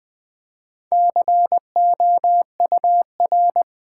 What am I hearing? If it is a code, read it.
Morse code, heard as a keyed tone, about 20 wpm, 700 Hz. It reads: COUR
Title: Build A Word - 2 Letters At A Time - 6-Letter Words - 20wpm